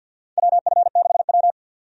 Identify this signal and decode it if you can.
Morse code, heard as a keyed tone, about 47 wpm, 700 Hz. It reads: WF6W
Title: USA Call Signs - 4 Characters - 50wpm